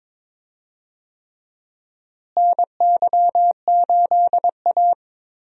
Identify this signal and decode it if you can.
Morse code, heard as a keyed tone, about 21 wpm, 700 Hz. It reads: NY8A
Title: USA Call Signs - 4 Characters - Repeated 3x - 22wpm